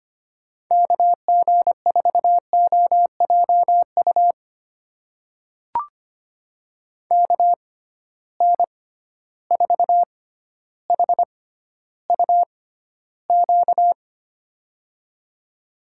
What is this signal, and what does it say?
Morse code, heard as a keyed tone, about 25 wpm, spaced as for 13 wpm, 700 Hz. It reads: KG4OJU KN4HUQ
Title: USA Call Signs - 6x Character Spacing - 25wpm